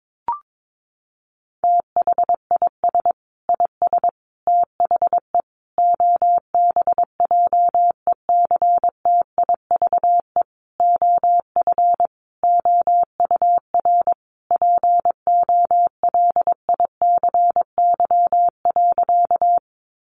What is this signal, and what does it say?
Morse code, heard as a keyed tone, about 22 wpm, 700 Hz. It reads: THIS IS THE OBJECTIVE OF OUR POLICY.
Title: Sentences from Top 1200 Words 22wpm